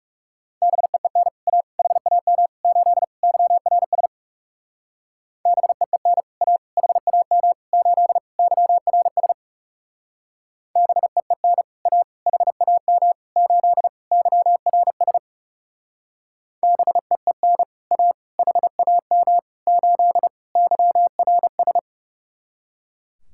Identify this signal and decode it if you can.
Morse code, heard as a keyed tone, about 38 wpm, 700 Hz. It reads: BEEN A HAM 8 YRS BEEN A HAM 8 YRS BEEN A HAM 8 YRS BEEN A HAM 8 YRS